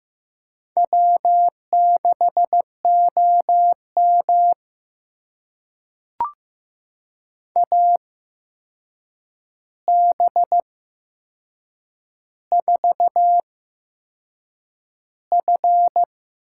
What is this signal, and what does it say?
Morse code, heard as a keyed tone, about 15 wpm, spaced as for 6 wpm, 700 Hz. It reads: W6OM AB4F
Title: USA Call Signs - 4 Characters - 8x Character Spacing - 15wpm